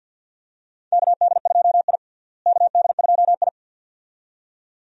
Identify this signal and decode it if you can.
Morse code, heard as a keyed tone, about 50 wpm, 700 Hz. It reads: KD2I KD2I
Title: USA Call Signs - 4 Characters - Repeated 2x - 50wpm